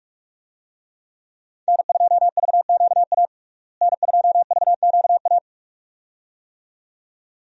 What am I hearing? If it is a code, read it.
Morse code, heard as a keyed tone, about 41 wpm, 700 Hz. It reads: N2VQA N2VQA
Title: USA Call Signs - 5 Characters - Repeated 2x - 45wpm